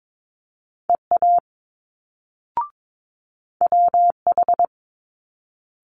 Morse code, heard as a keyed tone, about 22 wpm, 700 Hz. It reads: EA WH
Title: Multi-letter Phonemes Quick Fire 22wpm